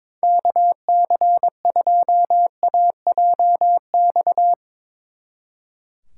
Morse code, heard as a keyed tone, about 24 wpm, 700 Hz. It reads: KC2AJX
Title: Speed Racing - USA 2x3 Call Signs 22wpm